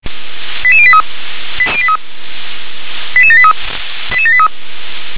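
Five-tone selective call (ZVEI), heard as quick runs of five tones, tones of about 70 ms. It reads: 88083 88783 88783
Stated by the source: ZVEI five-tone selective call 88081 received on an SDR.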